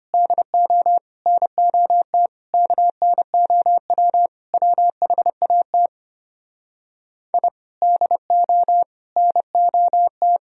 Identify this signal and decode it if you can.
Morse code, heard as a keyed tone, about 33 wpm, 700 Hz. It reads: DO NOT KNOW WHAT I DO NOT